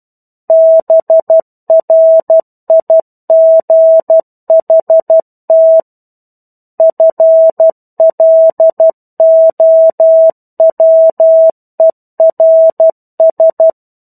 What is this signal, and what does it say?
Morse code, heard as a keyed tone, about 12 wpm, 650 Hz. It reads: BRIGHT FLOWERS
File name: 01-Chapter-01 12 WPM